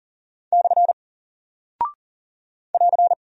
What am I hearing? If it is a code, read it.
Morse code, heard as a keyed tone, about 39 wpm, 700 Hz. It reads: / <AR>
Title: Single Character (Letter:Number:Punctuation:Prosign) Alt 40wpm